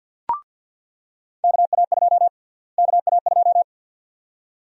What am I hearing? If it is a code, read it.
Morse code, heard as a keyed tone, about 49 wpm, 700 Hz. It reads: KA2 KA2
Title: USA Call Sign Prefixes - Repeated 2x - 50wpm